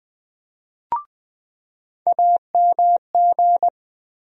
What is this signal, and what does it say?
Morse code, heard as a keyed tone, about 20 wpm, 700 Hz. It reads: AMG
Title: USA Call Sign Postfixes - 3 Letters 20wpm